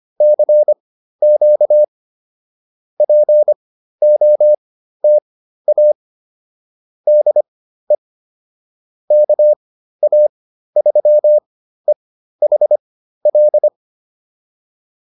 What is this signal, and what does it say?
Morse code, heard as a keyed tone, about 25 wpm, spaced as for 13 wpm, 600 Hz. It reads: CQ POTA DE KA3EHL